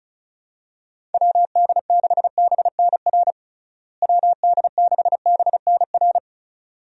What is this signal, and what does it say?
Morse code, heard as a keyed tone, about 36 wpm, 700 Hz. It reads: WD6BNR WD6BNR
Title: USA Call Signs - 6 Characters - Repeated 2x - 35wpm